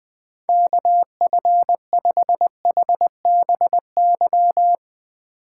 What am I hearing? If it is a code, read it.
Morse code, heard as a keyed tone, about 20 wpm, 700 Hz. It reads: KF5HBY